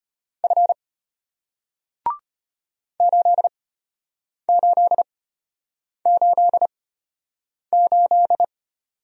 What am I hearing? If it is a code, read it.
Morse code, heard as a keyed tone, about 33 wpm, spaced as for 8 wpm, 700 Hz. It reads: F 8888